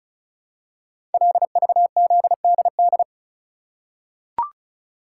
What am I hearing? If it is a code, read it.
Morse code, heard as a keyed tone, about 36 wpm, 690 Hz. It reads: LVZDD
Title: Code Groups - 5 Letters - 35wpm